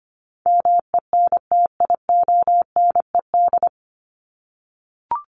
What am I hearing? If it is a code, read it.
Morse code, heard as a keyed tone, about 25 wpm, 700 Hz. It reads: MENTIONED
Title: ED Suffix - Top 100 Words with Base Words 25wpm